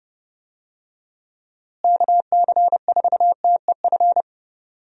Morse code, heard as a keyed tone, about 30 wpm, 700 Hz. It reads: KC4TEF
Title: USA Call Signs - 6 Characters - Repeated 2x - 30wpm